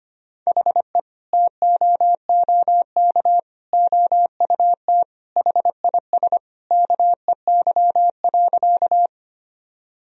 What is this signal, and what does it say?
Morse code, heard as a keyed tone, about 25 wpm, 700 Hz. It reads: HE TOOK OUT HIS KEY.
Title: Sentences from Top 1000 Words - Review Part 4 - 25wpm